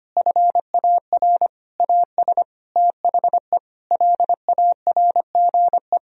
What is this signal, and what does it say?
Morse code, heard as a keyed tone, about 25 wpm, 700 Hz. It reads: FAR AS THE LARGE
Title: Sentences from Top 1000 Words - Review Part 4 - 25wpm